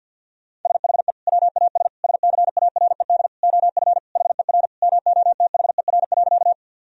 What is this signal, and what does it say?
Morse code, heard as a keyed tone, about 50 wpm, 700 Hz. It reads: SHE WAS SCARED OF HER MOTHER.
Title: Sentences from Top 1000 Words - Review Part 4 - 50wpm